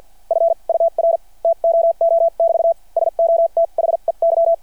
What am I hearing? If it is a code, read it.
Morse code, heard as a keyed tone, about 51 wpm, spaced as for 40 wpm, 660 Hz. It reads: FUR TOO <BT> SO THEY
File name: archived / CRSnet